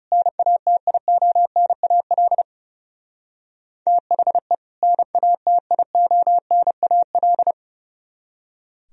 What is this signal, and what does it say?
Morse code, heard as a keyed tone, about 36 wpm, 700 Hz. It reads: NATIONAL THE NATIONAL
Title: Speed-Racing - Sets of 2 Words 30wpm